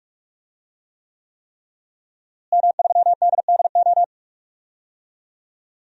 Morse code, heard as a keyed tone, about 41 wpm, 700 Hz. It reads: M3DDO
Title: DXCC-100 Call Signs - 5 Characters - Repeated 3x - 45wpm